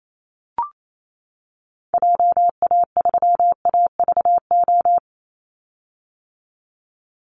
Morse code, heard as a keyed tone, about 28 wpm, 700 Hz. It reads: JA3AVO
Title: Call Signs - 28wpm